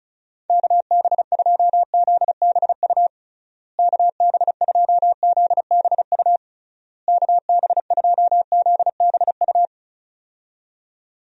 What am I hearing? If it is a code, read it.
Morse code, heard as a keyed tone, about 34 wpm, 700 Hz. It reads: KB2ZBU KB2ZBU KB2ZBU
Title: USA Call Signs - 6 Characters - Repeated 3x - 35wpm